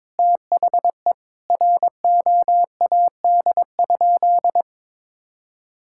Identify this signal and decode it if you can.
Morse code, heard as a keyed tone, about 22 wpm, 700 Hz. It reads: THE ROAD?
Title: Sentences from Top 1000 Words - Review Part 5 - 22wpm